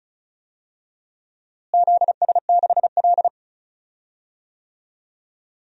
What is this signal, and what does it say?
Morse code, heard as a keyed tone, about 36 wpm, 700 Hz. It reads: ZS6L
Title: DXCC-100 Call Signs - 4 Characters - Repeated 2x - 35wpm